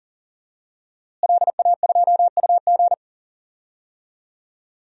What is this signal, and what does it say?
Morse code, heard as a keyed tone, about 37 wpm, 700 Hz. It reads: LA2UG
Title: DXCC-100 Call Signs - 5 Characters - Repeated 2x - 40wpm